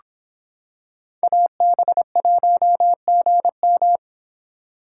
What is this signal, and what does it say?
Morse code, heard as a keyed tone, about 26 wpm, 700 Hz. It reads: AB1GM